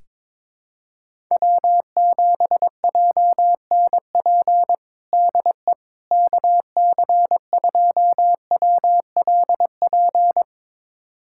Morse code, heard as a keyed tone, about 22 wpm, 700 Hz. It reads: W7JNP DE KC2WLP
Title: USA Callsign DE Callsign 22wpm